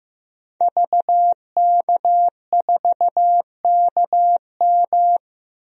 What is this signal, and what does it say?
Morse code, heard as a keyed tone, about 15 wpm, 700 Hz. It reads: VK4KM
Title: DXCC-100 Call Signs - 5 Characters - 15wpm